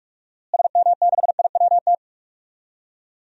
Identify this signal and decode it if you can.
Morse code, heard as a keyed tone, about 47 wpm, 700 Hz. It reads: SM6IWT
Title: DXCC-100 Call Signs - 6 Characters - Repeated 2x - 45wpm